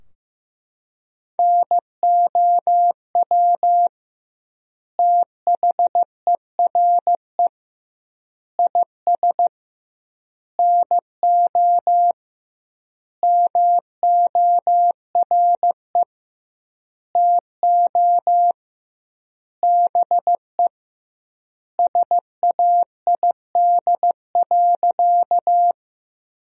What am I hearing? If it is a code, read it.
Morse code, heard as a keyed tone, about 16 wpm, 700 Hz. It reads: NOW THERE IS NO MORE TO BE SAID.
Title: 2x Word-Spacing - Sentences from Top 100 Words - 15wpm